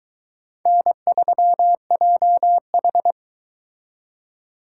Morse code, heard as a keyed tone, about 22 wpm, 700 Hz. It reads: N3JH